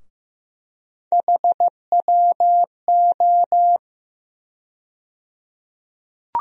Morse code, heard as a keyed tone, about 15 wpm, 700 Hz. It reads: HWO
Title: USA Call Sign Postfixes - 3 Letters 15wpm